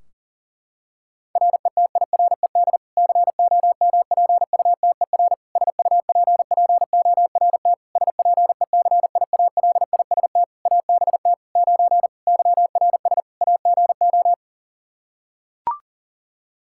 Morse code, heard as a keyed tone, about 37 wpm, 700 Hz. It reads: RETIRED COMPUTER SUPPORT SPECIALIST ABT 9 YRS AGO
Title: Practice QSO - Exchange 3 - More Info - 40wpm